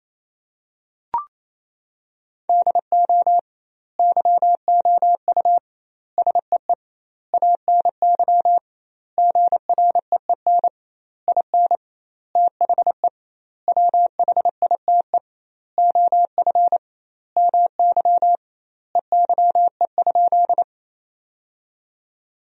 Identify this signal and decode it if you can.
Morse code, heard as a keyed tone, about 28 wpm, 700 Hz. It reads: DO YOU SEE ANY GREEN IN THE WHITE OF MY EYE?
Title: 2x Word-Spacing - Sentences from Top 500 Words - 28wpm